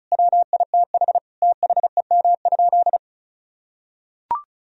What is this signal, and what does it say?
Morse code, heard as a keyed tone, about 36 wpm, 700 Hz. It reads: WITH THEM?
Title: Sentences from Top 1000 Words - Review Part 2 - 35wpm